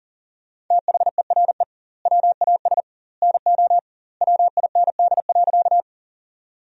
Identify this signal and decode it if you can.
Morse code, heard as a keyed tone, about 40 wpm, 700 Hz. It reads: THERE WAS NO WIND.
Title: 2x Word-Spacing - Sentences from Top 400 Words - 40wpm